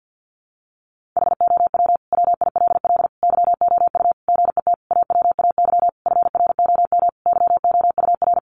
Morse code, heard as a keyed tone, about 50 wpm, 700 Hz. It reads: HOW WILL YOU GET AWAY FROM YOUR
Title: Sentences from Top 1500 Words 50wpm